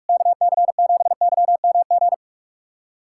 Morse code, heard as a keyed tone, about 46 wpm, 700 Hz. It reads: KC7YMG
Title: USA Callsign DE Callsign - 6 Characters - 45wpm